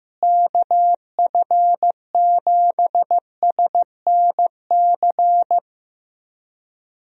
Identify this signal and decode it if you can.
Morse code, heard as a keyed tone, about 16 wpm, 700 Hz. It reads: KF7SNC
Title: USA Call Signs - 6 Characters - Repeated 2x - 15wpm